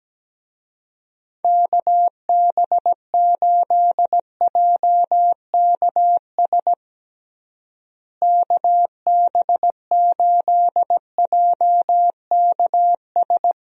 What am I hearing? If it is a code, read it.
Morse code, heard as a keyed tone, about 17 wpm, 700 Hz. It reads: KB8JKS KB8JKS
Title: USA Call Signs - 6 Characters - Repeated 2x - 17wpm